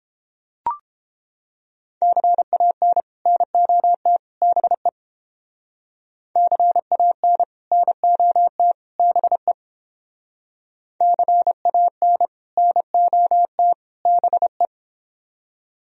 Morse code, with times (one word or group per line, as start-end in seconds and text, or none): 2.0-3.0 CAN
3.3-4.2 NOT
4.4-4.9 BE
6.4-7.4 CAN
7.7-8.7 NOT
9.0-9.5 BE
11.0-12.3 CAN
12.6-13.7 NOT
14.1-14.7 BE